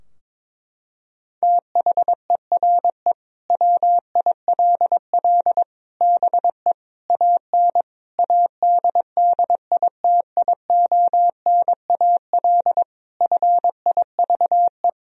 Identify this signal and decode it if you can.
Morse code, heard as a keyed tone, about 22 wpm, 700 Hz. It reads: THERE WILL BE AN ADDITIONAL FIVE
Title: Sentences from Top 1000 Words - Review Part 4 - 22wpm